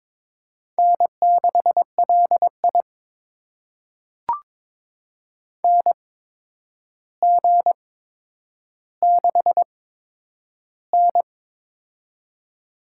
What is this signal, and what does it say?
Morse code, heard as a keyed tone, about 22 wpm, spaced as for 9 wpm, 700 Hz. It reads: N6LI NG6N
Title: USA Call Signs - 4 Characters - 8x Character Spacing - 22wpm